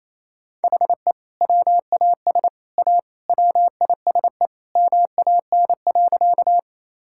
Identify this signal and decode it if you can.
Morse code, heard as a keyed tone, about 28 wpm, 700 Hz. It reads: HE WAS A WISE MAN.
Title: Sentences from Top 1600 Words 28wpm